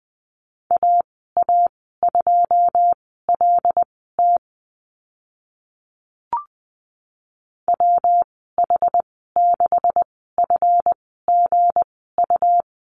Morse code, heard as a keyed tone, about 20 wpm, spaced as for 14 wpm, 700 Hz. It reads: AA2LT WH6FGU
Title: 193 Lesson 31g - Letter v - Call Signs 20wpm